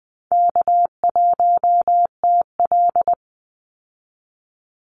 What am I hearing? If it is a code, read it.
Morse code, heard as a keyed tone, about 20 wpm, 700 Hz. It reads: K1TL
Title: Easy Call Signs Repeated 2x - 20wpm